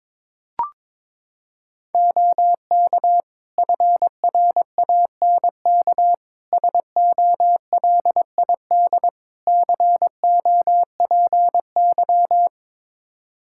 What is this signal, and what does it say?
Morse code, heard as a keyed tone, about 21 wpm, 700 Hz. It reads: OK FRANK SOLID COPY
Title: Practice QSO - Exchange 3 - More Info - 22wpm